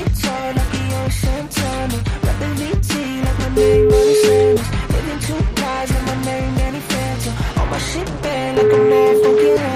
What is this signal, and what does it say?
Ringback tone (Europe, 425 Hz), with noise.